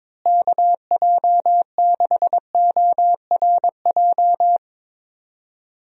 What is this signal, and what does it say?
Morse code, heard as a keyed tone, about 22 wpm, 700 Hz. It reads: KJ6ORJ
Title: USA Call Signs - 6 Characters - Repeated 2x - 22wpm